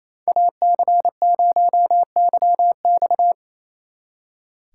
Morse code, heard as a keyed tone, about 34 wpm, 700 Hz. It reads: AC0YX
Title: Speed Racing - USA 2x2 Call Signs 28wpm